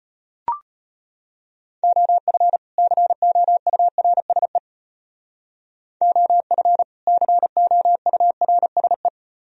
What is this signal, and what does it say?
Morse code, heard as a keyed tone, about 33 wpm, 700 Hz. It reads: OF COURSE OF COURSE